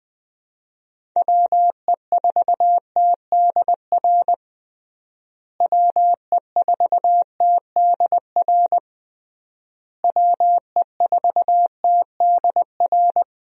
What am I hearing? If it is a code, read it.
Morse code, heard as a keyed tone, about 21 wpm, 700 Hz. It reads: WE4TDR WE4TDR WE4TDR
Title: USA Call Signs - 6 Characters - Repeated 3x - 20wpm